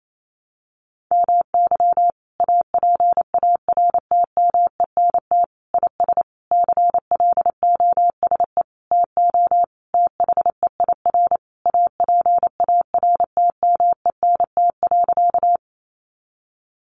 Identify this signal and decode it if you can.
Morse code, heard as a keyed tone, about 28 wpm, 700 Hz. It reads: MY APARTMENT IS CLOSE TO THEIR APARTMENT.
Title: Sentences from Top 900 Words 28wpm